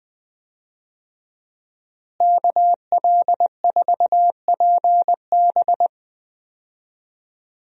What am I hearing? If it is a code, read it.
Morse code, heard as a keyed tone, about 20 wpm, 700 Hz. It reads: KL4PB
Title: USA Call Signs - 5 Characters - Repeated 3x - 20wpm